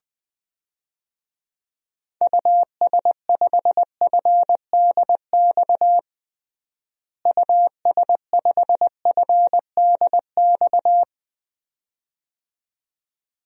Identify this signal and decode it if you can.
Morse code, heard as a keyed tone, about 20 wpm, 700 Hz. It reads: US5FDX US5FDX
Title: DXCC-100 Call Signs - 6 Characters - Repeated 2x - 20wpm